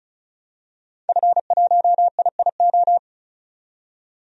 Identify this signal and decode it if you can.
Morse code, heard as a keyed tone, about 36 wpm, 700 Hz. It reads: F1IIO
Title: DXCC-100 Call Signs - 5 Characters - 35wpm